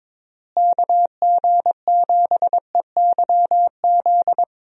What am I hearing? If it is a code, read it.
Morse code, heard as a keyed tone, about 22 wpm, 700 Hz. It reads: KG7EYZ
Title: USA Call Signs - 6 Characters - Repeated 3x - 22wpm